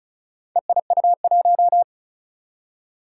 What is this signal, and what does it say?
Morse code, heard as a keyed tone, about 36 wpm, 700 Hz. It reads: EIU1
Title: Four Challenging Letter-Number Combinations 35wpm